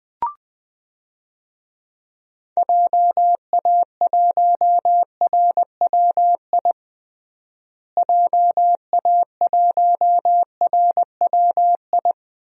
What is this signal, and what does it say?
Morse code, heard as a keyed tone, about 21 wpm, 700 Hz. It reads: JA1RWI JA1RWI
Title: USA Call Signs - 6 Characters - Repeated 2x - 20wpm